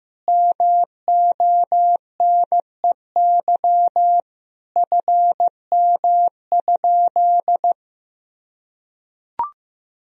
Morse code, heard as a keyed tone, about 15 wpm, 700 Hz. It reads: MONEY FM?
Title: Sentences from Top 400 Words and Abbreviations 15wpm